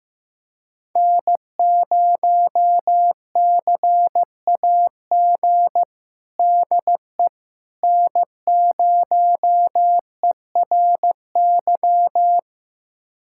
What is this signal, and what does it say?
Morse code, heard as a keyed tone, about 15 wpm, 700 Hz. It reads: N0CAG DE N0ERY
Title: USA Callsign DE Callsign - 5 Characters - 15wpm